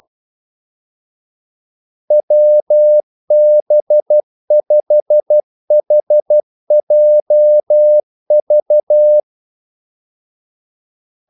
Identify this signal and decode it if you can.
Morse code, heard as a keyed tone, about 12 wpm, 600 Hz. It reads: WB5HJV
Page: 2 ekstra ordmellemrum mellem kaldesignalerne